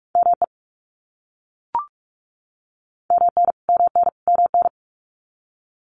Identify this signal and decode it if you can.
Morse code, heard as a keyed tone, about 44 wpm, 700 Hz. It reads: ME MN MN MN
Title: US States Repeated 3x 45wpm